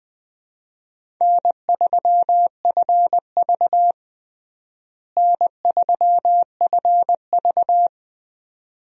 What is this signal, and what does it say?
Morse code, heard as a keyed tone, about 20 wpm, 700 Hz. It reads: N3FV N3FV
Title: USA Call Signs - 4 Characters - Repeated 3x - 20wpm